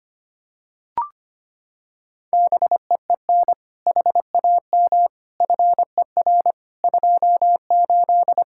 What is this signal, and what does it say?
Morse code, heard as a keyed tone, about 25 wpm, 700 Hz. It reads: BEEN HAM FER 28
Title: K1USN SST QSO Practice 25wpm